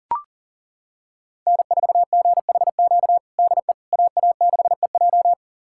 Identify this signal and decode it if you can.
Morse code, heard as a keyed tone, about 40 wpm, 700 Hz. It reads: N4GHQ DE AA6EJ
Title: USA Callsign DE Callsign - 5 Characters - 40wpm